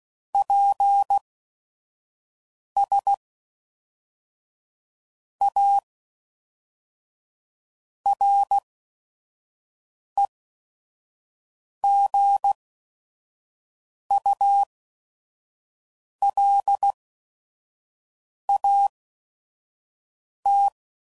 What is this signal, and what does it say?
Morse code, heard as a keyed tone, about 16 wpm, spaced as for 5 wpm, 800 Hz. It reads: PS A REGULAT
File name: CW Code Groups @5WPM